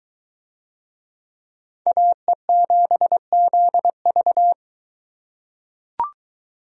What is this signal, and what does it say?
Morse code, heard as a keyed tone, about 22 wpm, 700 Hz. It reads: AE7ZV